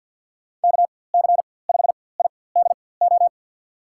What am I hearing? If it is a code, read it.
Morse code, heard as a keyed tone, about 50 wpm, spaced as for 23 wpm, 700 Hz. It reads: KC5IDO
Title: USA Call Signs - 6 Characters - 4x Character Spacing - 50wpm